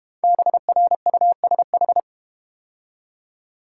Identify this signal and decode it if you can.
Morse code, heard as a keyed tone, about 32 wpm, 700 Hz. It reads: BRUSH